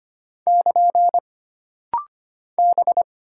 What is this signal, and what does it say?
Morse code, heard as a keyed tone, about 25 wpm, 700 Hz. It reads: <KN> B
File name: Single Character (Letter:Number:Punctuation:Prosign) - Rapid-Fire - 25wpm